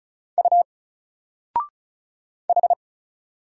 Morse code, heard as a keyed tone, about 34 wpm, 700 Hz. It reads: U H
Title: Single Letter-Number 35wpm